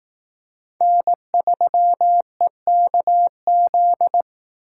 Morse code, heard as a keyed tone, about 18 wpm, 700 Hz. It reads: N3EKZ
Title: Speed Racing - USA 1x3 Call Signs 15wpm